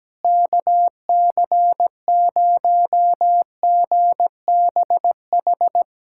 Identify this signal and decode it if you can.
Morse code, heard as a keyed tone, about 17 wpm, 700 Hz. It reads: KC0GBH
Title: USA Callsign DE Callsign 17wpm